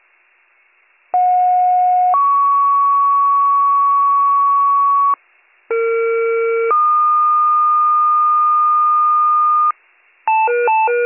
Two-tone page, heard as a one-second tone, then a three-second tone, about 730 then 1120 Hz, about 460 then 1220 Hz. It is heard from 1.1 to 9.7 s.